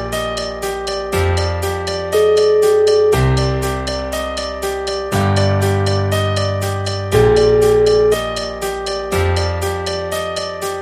Ringback tone (Europe, 425 Hz), with noise.